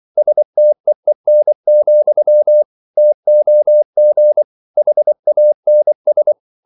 Morse code, heard as a keyed tone, about 24 wpm, 600 Hz. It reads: STEEN, TOG HANS
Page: Ferie 24 wpm